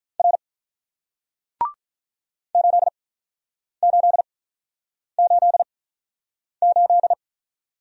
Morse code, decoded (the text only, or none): R 8888